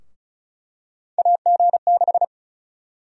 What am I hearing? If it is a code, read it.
Morse code, heard as a keyed tone, about 34 wpm, 700 Hz. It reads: AG6
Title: USA Call Sign Prefixes - Repeated 2x - 35wpm